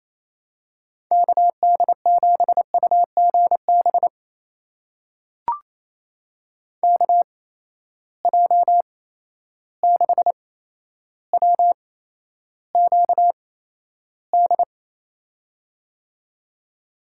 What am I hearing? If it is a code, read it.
Morse code, heard as a keyed tone, about 28 wpm, spaced as for 12 wpm, 700 Hz. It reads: KD7UGB KJ6WQD
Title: USA Call Signs - 6 Characters - 8x Character Spacing - 28wpm